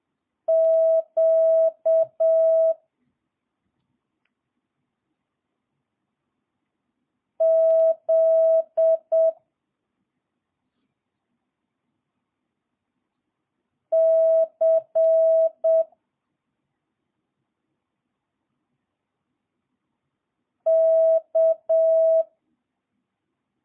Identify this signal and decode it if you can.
Morse code, heard as a keyed tone, about 7 wpm, 650 Hz. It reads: Q Z C K
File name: jh8cbh / 2022cw